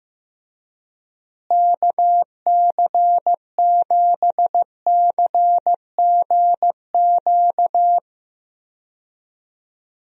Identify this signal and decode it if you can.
Morse code, heard as a keyed tone, about 15 wpm, 700 Hz. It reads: KC7CGQ
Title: USA Call Signs - 6 Characters - Repeated 2x - 15wpm